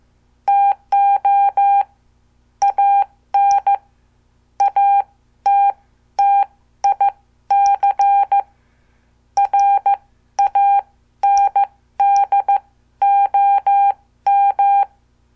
Morse code, heard as a keyed tone, about 14 wpm, spaced as for 12 wpm, 800 Hz. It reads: TO AN ATTIC RANDOM
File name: morse / qso